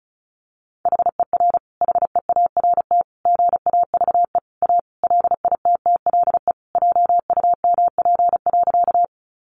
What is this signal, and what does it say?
Morse code, heard as a keyed tone, about 34 wpm, 700 Hz. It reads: HER HEART GAVE A LITTLE JUMP.
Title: Sentences from Top 1300 Words 35wpm